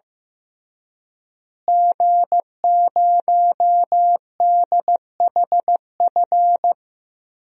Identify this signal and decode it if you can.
Morse code, heard as a keyed tone, about 15 wpm, 700 Hz. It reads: G0DHF
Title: DXCC-100 Call Signs - 5 Characters - Repeated 2x - 15wpm